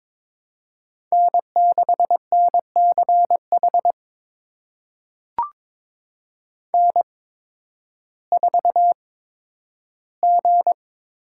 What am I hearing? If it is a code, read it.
Morse code, heard as a keyed tone, about 22 wpm, spaced as for 12 wpm, 700 Hz. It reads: N6NCH N4G
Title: USA Call Signs - 5 Characters - 8x Character Spacing - 22wpm